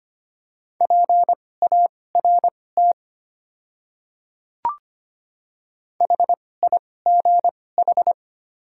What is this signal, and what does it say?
Morse code, heard as a keyed tone, about 25 wpm, spaced as for 18 wpm, 700 Hz. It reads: PART HIGH
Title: 2x Character Spacing - Top 500 Words - 25wpm